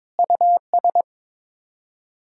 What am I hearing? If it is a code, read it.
Morse code, heard as a keyed tone, about 22 wpm, 700 Hz. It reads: US
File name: practice / 22wpm / Build A Phrase - 5-Word Phrases - 22wpm